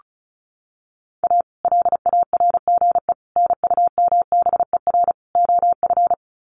Morse code, heard as a keyed tone, about 34 wpm, 700 Hz. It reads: A LARGE NUMBER OF
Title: Sets of 4 Words 35wpm